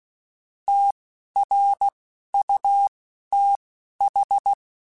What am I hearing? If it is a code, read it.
Morse code, heard as a keyed tone, about 16 wpm, spaced as for 11 wpm, 800 Hz. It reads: TRUTH
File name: Technical Subjects / Technical Subjects-mp3s / Gates Intro @10